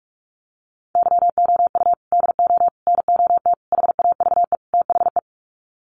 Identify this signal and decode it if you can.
Morse code, heard as a keyed tone, about 44 wpm, 700 Hz. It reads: YOU DO NOT HAVE THE
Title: Sets of 5 Words 45wpm